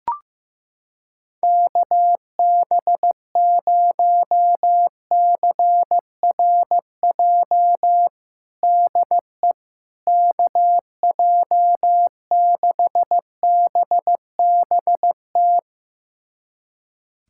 Morse code, heard as a keyed tone, about 15 wpm, 700 Hz. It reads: KB0CRJ DE KJ6BBT
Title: USA Callsign DE Callsign - 6 Characters - 15wpm